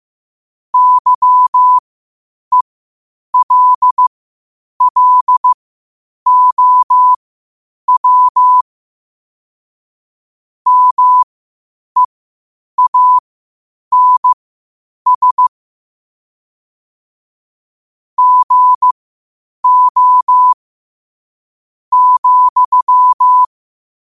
Morse code, heard as a keyed tone, about 15 wpm, spaced as for 8 wpm, 1000 Hz. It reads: YELLOW MEANS GO,
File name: qotdfiles / 8wpm